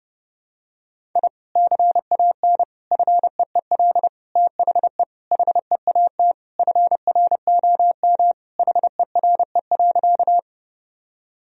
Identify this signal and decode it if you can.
Morse code, heard as a keyed tone, about 30 wpm, 700 Hz. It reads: I CAN FEEL THE HEAT FROM HERE.
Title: Sentences from Top 1000 Words - Review Part 3 - 30wpm